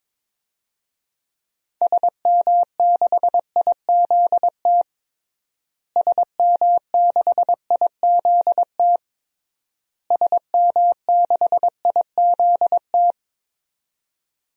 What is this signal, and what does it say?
Morse code, heard as a keyed tone, about 22 wpm, 700 Hz. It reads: SM6IZT SM6IZT SM6IZT
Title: DXCC-100 Call Signs - 6 Characters - Repeated 3x - 22wpm